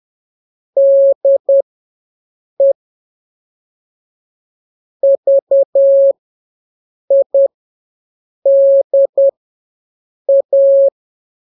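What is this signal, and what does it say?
Morse code, heard as a keyed tone, about 10 wpm, spaced as for 6 wpm, 550 Hz. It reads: DE VIDA
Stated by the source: Telegraphy radio challenge for scouts in the Jamboree-On-The-Air.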